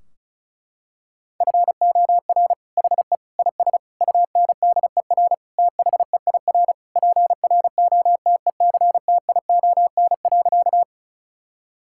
Morse code, heard as a keyed tone, about 34 wpm, 700 Hz. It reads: FOR HE IS UNDER THEIR PROTECTION.
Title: Sentences from Top 1600 Words 35wpm